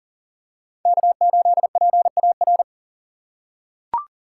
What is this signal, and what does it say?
Morse code, heard as a keyed tone, about 40 wpm, 700 Hz. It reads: K8PAR
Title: USA Call Signs - 5 Characters - 2x Character Spacing - 40wpm